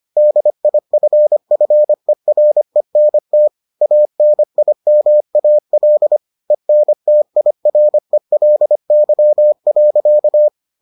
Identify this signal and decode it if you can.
Morse code, heard as a keyed tone, about 25 wpm, 600 Hz. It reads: DIFFERENT ANIMAL ENTIRELY.